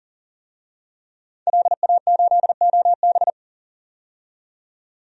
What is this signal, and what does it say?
Morse code, heard as a keyed tone, about 40 wpm, 700 Hz. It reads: LA8OB
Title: DXCC-100 Call Signs - 5 Characters - 40wpm